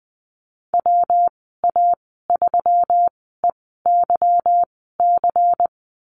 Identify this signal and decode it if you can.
Morse code, heard as a keyed tone, about 20 wpm, spaced as for 15 wpm, 700 Hz. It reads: WA3EYC
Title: 167 Lesson 27g - Number 7 - Call Signs 20wpm